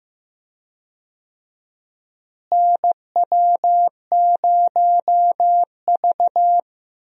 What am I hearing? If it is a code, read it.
Morse code, heard as a keyed tone, about 15 wpm, 700 Hz. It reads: NW0V
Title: USA Call Signs - 4 Characters - Repeated 2x - 15wpm